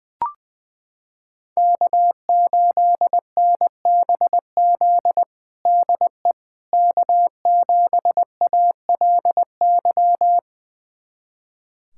Morse code, heard as a keyed tone, about 20 wpm, 700 Hz. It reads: K8NBZ DE K7ALY
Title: USA Callsign DE Callsign - 5 Characters - 20wpm